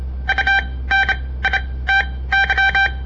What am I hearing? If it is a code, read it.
Morse code, heard as a keyed tone, about 27 wpm, spaced as for 18 wpm, 1550 Hz. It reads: UNITY
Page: On the air Morse code practice from the Southern Montana Amatuer Radio Association; all code, no read back